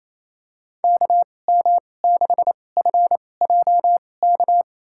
Morse code, heard as a keyed tone, about 28 wpm, spaced as for 20 wpm, 700 Hz. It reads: KM6FJK
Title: USA Call Signs - 6 Characters - 2x Character Spacing - 28wpm